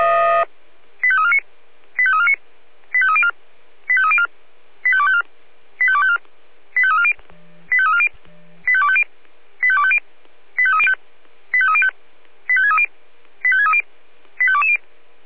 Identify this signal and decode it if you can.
Five-tone selective call (ZVEI), heard as quick runs of five tones, tones of about 70 ms. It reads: 85349 85349 85304 85304 85254 85254 85340 85340 85250 85250 85305 85305 85630 85630 85220
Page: /496/SDR/ZVEI/